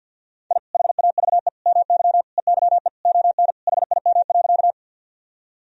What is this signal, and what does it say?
Morse code, heard as a keyed tone, about 50 wpm, 700 Hz. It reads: I HAVE MY EYE ON HIM.
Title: Sentences from Top 1000 Words - Review Part 1 - 50wpm